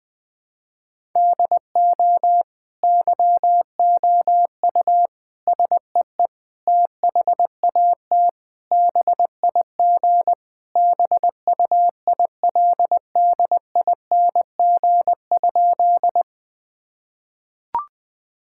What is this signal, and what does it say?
Morse code, heard as a keyed tone, about 20 wpm, 700 Hz. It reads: DO YOU SEE THAT BIG BUILDING?
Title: Sentences from Top 1000 Words - Review Part 5 - 20wpm